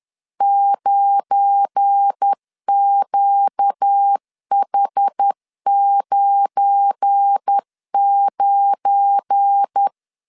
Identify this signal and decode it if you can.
Morse code, heard as a keyed tone, about 10 wpm, 800 Hz. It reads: 9QH99
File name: LEZIONI / 7 GF9Z / 7 GF9 10wpm 2747